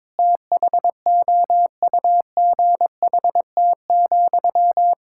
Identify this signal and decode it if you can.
Morse code, heard as a keyed tone, about 22 wpm, 700 Hz. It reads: THOUGHT,
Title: Sentences from Top 1600 Words 22wpm